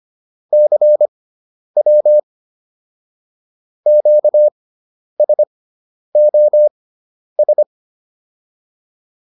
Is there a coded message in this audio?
Yes: Morse code from 0.5 to 7.6 s, heard as a keyed tone, about 25 wpm, spaced as for 10 wpm, 600 Hz. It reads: CW QSOS